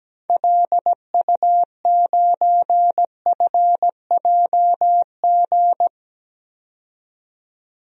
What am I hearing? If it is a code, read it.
Morse code, heard as a keyed tone, about 17 wpm, 700 Hz. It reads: LU9FJG
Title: DXCC-100 Call Signs - 6 Characters - 17wpm